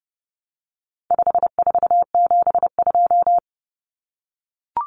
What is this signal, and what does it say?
Morse code, heard as a keyed tone, about 30 wpm, 700 Hz. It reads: 5472
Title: Common Cut Numbers - 4-Digit 30wpm